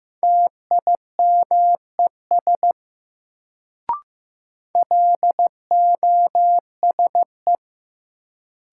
Morse code, heard as a keyed tone, about 15 wpm, 700 Hz. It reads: TIMES LOSE
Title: Sentences from Top 1100 Words - New Word Drill Alt - 15wpm